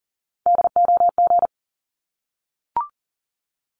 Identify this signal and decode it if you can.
Morse code, heard as a keyed tone, about 40 wpm, 700 Hz. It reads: DOG
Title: Top 500 Words 40wpm